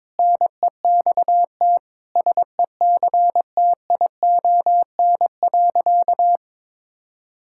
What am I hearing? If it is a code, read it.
Morse code, heard as a keyed tone, about 22 wpm, 700 Hz. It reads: NEXT SECTION.
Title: Sentences from Top 1000 Words - Review Part 5 - 22wpm